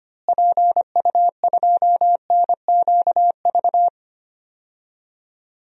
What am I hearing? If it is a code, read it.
Morse code, heard as a keyed tone, about 25 wpm, 700 Hz. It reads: PU2NQV
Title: DXCC-100 Call Signs - 6 Characters - Repeated 3x - 25wpm